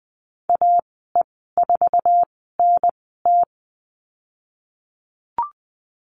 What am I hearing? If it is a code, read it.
Morse code, heard as a keyed tone, about 20 wpm, spaced as for 14 wpm, 700 Hz. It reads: AE4NT
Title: 048 Lesson 9g - Number 4 - Call Signs 20wpm